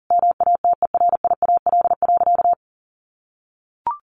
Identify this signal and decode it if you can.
Morse code, heard as a keyed tone, about 40 wpm, 700 Hz. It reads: MATERIAL.
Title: Sentences from Top 1100 Words 40wpm